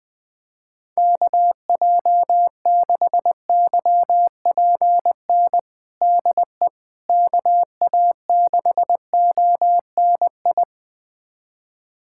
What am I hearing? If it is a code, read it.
Morse code, heard as a keyed tone, about 20 wpm, 700 Hz. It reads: KJ6YPN DE KA6ONI
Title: USA Callsign DE Callsign - 6 Characters - 20wpm